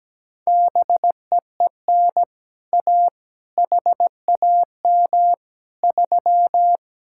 Morse code, heard as a keyed tone, about 17 wpm, 700 Hz. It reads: BEEN A HAM 3
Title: Speed-Racing Common QSO Phrases Encore - 17wpm